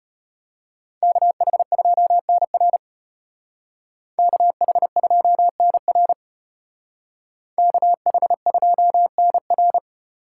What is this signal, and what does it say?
Morse code, heard as a keyed tone, about 30 wpm, 700 Hz. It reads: KH2NR KH2NR KH2NR
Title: Speed Racing - USA 2x2 Call Signs 25wpm